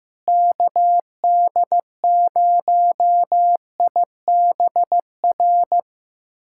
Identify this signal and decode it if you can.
Morse code, heard as a keyed tone, about 15 wpm, 700 Hz. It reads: KD0IBR
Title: USA Callsign DE Callsign 15wpm